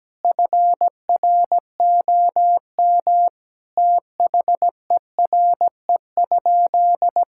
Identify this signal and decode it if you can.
Morse code, heard as a keyed tone, about 17 wpm, 700 Hz. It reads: FROM THERE?
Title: Sentences from Top 1000 Words - Review Part 6 - 17wpm